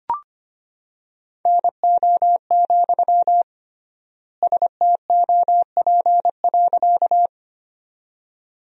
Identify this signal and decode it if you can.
Morse code, heard as a keyed tone, about 25 wpm, 700 Hz. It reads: NO, STOP.
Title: 3x Word-Spacing - Sentences from Top 300 Words - 25wpm